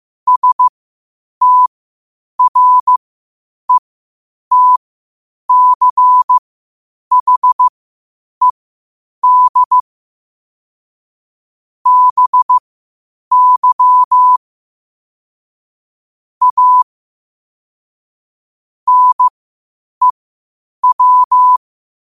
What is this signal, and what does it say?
Morse code, heard as a keyed tone, about 15 wpm, spaced as for 8 wpm, 1000 Hz. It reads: STRETCHED BY A NEW
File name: qotdfiles / 8wpm